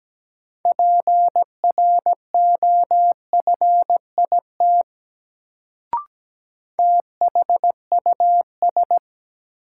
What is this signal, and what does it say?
Morse code, heard as a keyed tone, about 17 wpm, 700 Hz. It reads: PROFIT THUS
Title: Sentences from Top 800 Words - New Word Drill - 17wpm